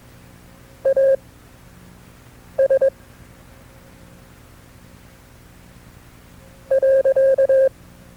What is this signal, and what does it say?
Morse code, heard as a keyed tone, about 21 wpm, spaced as for 6 wpm, 540 Hz. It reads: AS .
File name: FreeFiles / CW-practice / KB6MT-tape2-side2